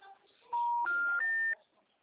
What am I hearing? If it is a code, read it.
Telephone call progress tone, heard as a telephone network's tone. Special information tone.